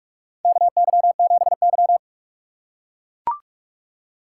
Morse code, heard as a keyed tone, about 44 wpm, 700 Hz. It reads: KY7Y
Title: USA Call Signs - 4 Characters - 6x Character Spacing - 45wpm